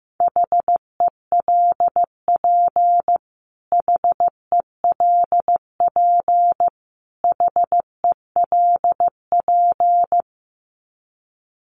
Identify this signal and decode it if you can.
Morse code, heard as a keyed tone, about 15 wpm, 700 Hz. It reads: HELP HELP HELP
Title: Top 500 Words Repeated 15wpm